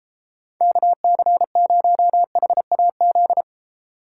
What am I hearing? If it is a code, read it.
Morse code, heard as a keyed tone, about 34 wpm, 700 Hz. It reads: KC0HAZ
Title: Speed Racing - USA 2x3 Call Signs 28wpm